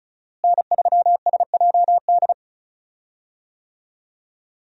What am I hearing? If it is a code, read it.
Morse code, heard as a keyed tone, about 36 wpm, 700 Hz. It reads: N3SJD
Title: USA Call Signs - 5 Characters - Repeated 3x - 35wpm